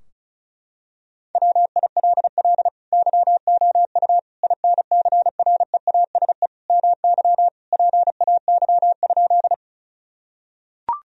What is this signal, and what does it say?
Morse code, heard as a keyed tone, about 34 wpm, 700 Hz. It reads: WILL YOU INCREASE MY PAY?
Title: Sentences from Top 1000 Words - Review Part 4 - 35wpm